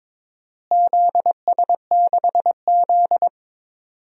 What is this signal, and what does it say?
Morse code, heard as a keyed tone, about 22 wpm, 700 Hz. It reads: ZS6Z
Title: DXCC-100 Call Signs - 4 Characters - Repeated 3x - 22wpm